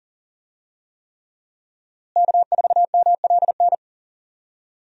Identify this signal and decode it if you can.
Morse code, heard as a keyed tone, about 40 wpm, 700 Hz. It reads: K4MLN
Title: USA Call Signs - 5 Characters - Repeated 3x - 40wpm